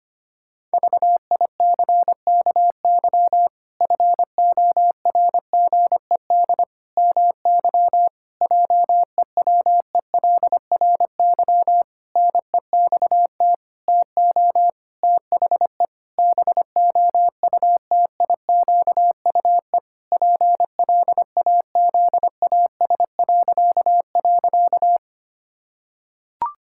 Morse code, heard as a keyed tone, about 25 wpm, 700 Hz. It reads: VICKY FORGED MY JEWELRY NEXT TO THE BOUTIQUE PLAZAS..
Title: Pangram Challenge v2 - 25wpm